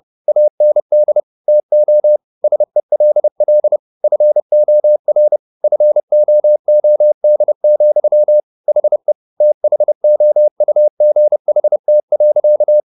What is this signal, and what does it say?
Morse code, heard as a keyed tone, about 30 wpm, 600 Hz. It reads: AND TO SELL FOR FOOD, HE THOUGHT.